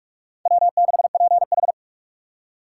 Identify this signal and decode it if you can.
Morse code, heard as a keyed tone, about 46 wpm, 700 Hz. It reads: W6PH
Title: USA Call Signs - 4 Characters - 6x Character Spacing - 45wpm